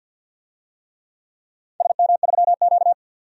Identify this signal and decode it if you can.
Morse code, heard as a keyed tone, about 45 wpm, 700 Hz. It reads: SM3Q
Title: DXCC-100 Call Signs - 4 Characters - 50wpm